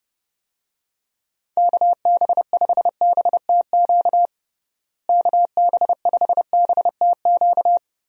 Morse code, heard as a keyed tone, about 30 wpm, 700 Hz. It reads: KB5BTQ KB5BTQ
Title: USA Call Signs - 6 Characters - Repeated 3x - 30wpm